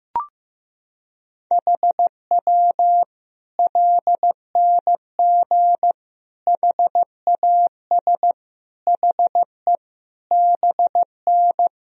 Morse code, heard as a keyed tone, about 15 wpm, 700 Hz. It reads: HW LNG HAS HE BN
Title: Sentences from Top 100 Words and Abbreviations 15wpm